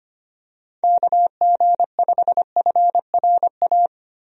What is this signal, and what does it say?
Morse code, heard as a keyed tone, about 25 wpm, 700 Hz. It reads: KG5FRA
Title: USA Call Signs - 6 Characters - 25wpm